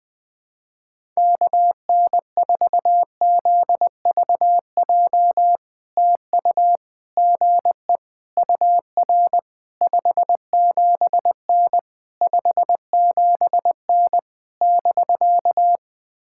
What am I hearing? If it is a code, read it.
Morse code, heard as a keyed tone, about 20 wpm, 700 Hz. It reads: KN4ZVJ TU GE UR 57N 57N <BK>
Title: POTA QSO Practice - 20wpm